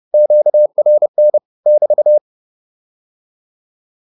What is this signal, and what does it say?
Morse code, heard as a keyed tone, about 30 wpm, 600 Hz. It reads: QRN <BT>